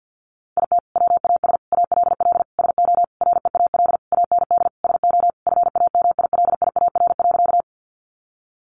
Morse code, heard as a keyed tone, about 50 wpm, 700 Hz. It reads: IT WAS ALL SO REAL AND SO FAMILIAR.
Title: Sentences from Top 1000 Words 50wpm